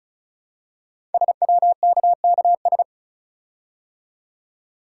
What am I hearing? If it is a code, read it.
Morse code, heard as a keyed tone, about 36 wpm, 700 Hz. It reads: SWKKS
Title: Code Groups - 5 Letters - 35wpm